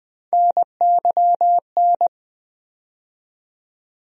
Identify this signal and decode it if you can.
Morse code, heard as a keyed tone, about 20 wpm, 700 Hz. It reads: NYN
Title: USA Call Sign Postfixes - 3 Letters 20wpm